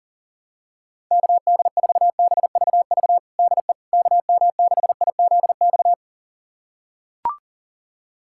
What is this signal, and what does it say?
Morse code, heard as a keyed tone, about 40 wpm, 700 Hz. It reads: KD4BVV DE KM6IZX
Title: USA Callsign DE Callsign - 6 Characters - 40wpm